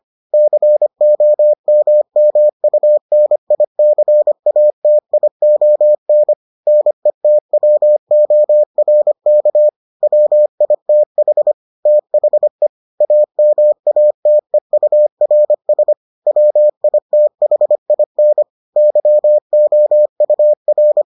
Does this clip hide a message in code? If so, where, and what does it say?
Yes: Morse code from 0.3 to 21.0 s, heard as a keyed tone, about 25 wpm, 600 Hz. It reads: COMMUNICATION NETWORK WITH THE AMATEURS WITHIN YOUR